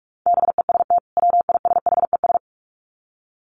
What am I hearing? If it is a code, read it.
Morse code, heard as a keyed tone, about 44 wpm, 700 Hz. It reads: BEST WISHES
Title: Common QSO Phrases 45wpm